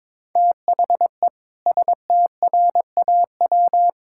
Morse code, heard as a keyed tone, about 22 wpm, 700 Hz. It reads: THE STRAW
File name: practice / 22wpm / Sets of 7 Words - 22wpm